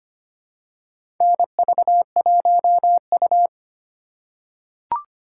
Morse code, heard as a keyed tone, about 25 wpm, 700 Hz. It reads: NV1U
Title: K1USN SST QSO Practice - 25wpm